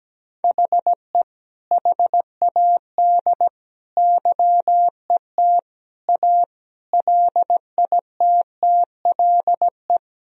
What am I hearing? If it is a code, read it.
Morse code, heard as a keyed tone, about 17 wpm, 700 Hz. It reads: HE HAD YET A LITTLE
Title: Sentences from Top 1000 Words - Review Part 2 - 17wpm